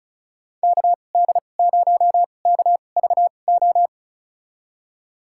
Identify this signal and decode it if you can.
Morse code, heard as a keyed tone, about 34 wpm, spaced as for 25 wpm, 700 Hz. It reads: KD0KVO
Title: USA Call Signs - 2x Character Spacing - 35wpm